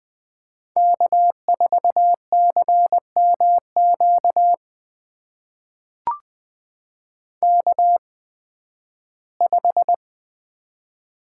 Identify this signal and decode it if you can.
Morse code, heard as a keyed tone, about 20 wpm, spaced as for 13 wpm, 700 Hz. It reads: K4CMQ K5
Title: USA Call Signs - 5 Characters - 8x Character Spacing - 20wpm